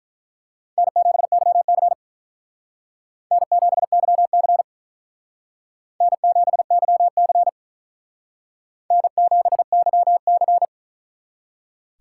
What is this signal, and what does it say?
Morse code, heard as a keyed tone, about 43 wpm, 700 Hz. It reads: N7YC N7YC N7YC N7YC
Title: Speed Racing - USA Call Signs 35wpm